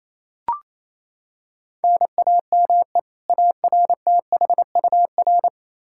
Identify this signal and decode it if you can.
Morse code, heard as a keyed tone, about 28 wpm, 700 Hz. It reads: NAME ARTHUR
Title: Practice QSO - Exchange 1 - Beginning - 28wpm